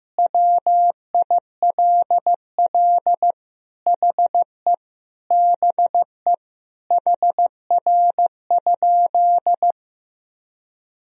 Morse code, heard as a keyed tone, about 15 wpm, 700 Hz. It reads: WILL HE BE HR?
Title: Sentences from Top 400 Words and Abbreviations 15wpm